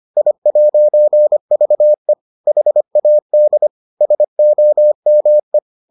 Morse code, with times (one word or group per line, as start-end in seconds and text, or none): 0.2-2.1 I'VE
2.5-3.7 HAD
4.0-5.6 SOME